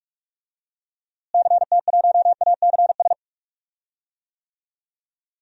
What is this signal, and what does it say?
Morse code, heard as a keyed tone, about 46 wpm, 700 Hz. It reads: CT1ACS
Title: DXCC-100 Call Signs - 6 Characters - Repeated 2x - 45wpm